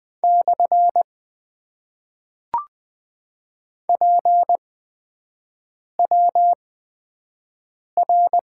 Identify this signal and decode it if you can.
Morse code, heard as a keyed tone, about 20 wpm, 700 Hz. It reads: / P W R
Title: Top 90 QSO Elements - 8x Character Spacing - 20wpm